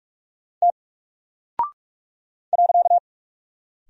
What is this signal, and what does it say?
Morse code, heard as a keyed tone, about 44 wpm, 700 Hz. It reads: T .
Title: Single Character (Letter:Number:Punctuation:Prosign) Alt 45wpm